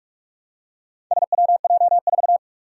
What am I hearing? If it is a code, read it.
Morse code, heard as a keyed tone, about 44 wpm, 700 Hz. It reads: SWJ4
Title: Four Challenging Letter-Number Combinations 45wpm